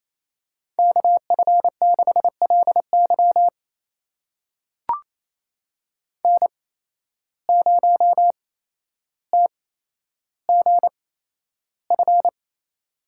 Morse code, heard as a keyed tone, about 28 wpm, spaced as for 12 wpm, 700 Hz. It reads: KF6LY N0TGF
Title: USA Call Signs - 5 Characters - 8x Character Spacing - 28wpm